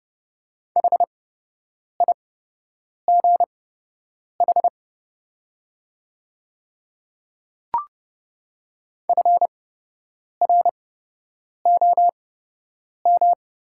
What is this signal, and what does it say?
Morse code, heard as a keyed tone, about 29 wpm, spaced as for 8 wpm, 700 Hz. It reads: HIGH FROM
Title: 8x Character Spacing - Top 200 Words - 30wpm